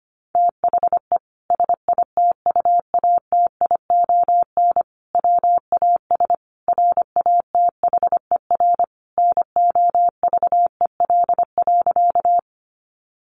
Morse code, heard as a keyed tone, about 25 wpm, 700 Hz. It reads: THE SITUATION WAS RATHER NOVEL.
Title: Sentences from Top 1300 Words 25wpm